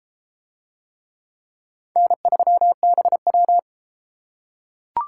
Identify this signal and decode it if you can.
Morse code, heard as a keyed tone, about 33 wpm, 700 Hz. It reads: N3BW